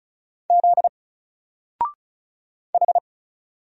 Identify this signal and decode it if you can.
Morse code, heard as a keyed tone, about 36 wpm, 700 Hz. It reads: Z H
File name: Single-Letter-Number-35wpm